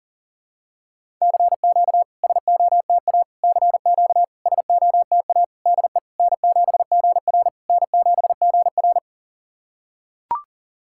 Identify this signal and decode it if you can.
Morse code, heard as a keyed tone, about 40 wpm, 690 Hz. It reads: CQ SOTA CQ SOTA DE N7GR N7GR
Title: SOTA QSO Practice 40wpm